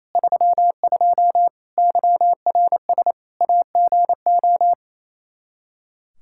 Morse code, heard as a keyed tone, about 28 wpm, 700 Hz. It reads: 32 YRS AGO
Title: Practice QSO - Exchange 3 - More Info - 28wpm